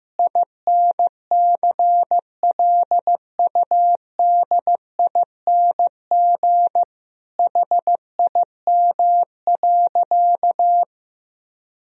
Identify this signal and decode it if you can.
Morse code, heard as a keyed tone, about 16 wpm, 700 Hz. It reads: INCLUDING HIM.
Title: Sentences from Top 1000 Words - Review Part 4 - 15wpm